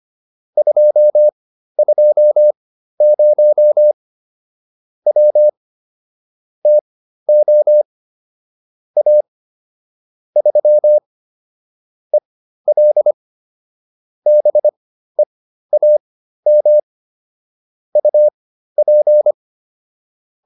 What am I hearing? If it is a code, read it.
Morse code, heard as a keyed tone, about 25 wpm, spaced as for 13 wpm, 600 Hz. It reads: 220 W TO A 3 EL BEAM UP
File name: QSO310 13